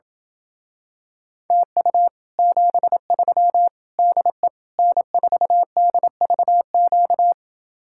Morse code, heard as a keyed tone, about 28 wpm, 700 Hz. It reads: TU 73 DE N4DVQ